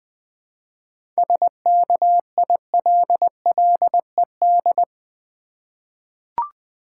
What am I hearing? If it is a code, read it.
Morse code, heard as a keyed tone, about 20 wpm, 700 Hz. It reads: SKILLED
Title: 7 Letter Words - First 3 Letters Given - 20wpm